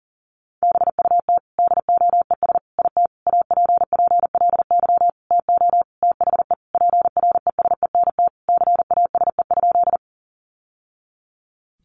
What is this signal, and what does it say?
Morse code, heard as a keyed tone, about 40 wpm, 700 Hz. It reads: BUT DOES IT APPLY TO THE PRESENT CASE?
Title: Sentences from Top 1200 Words 40wpm